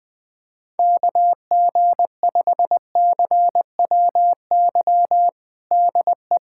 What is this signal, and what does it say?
Morse code, heard as a keyed tone, about 20 wpm, 700 Hz. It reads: KG5CWY DE
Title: USA Callsign DE Callsign - 6 Characters - 20wpm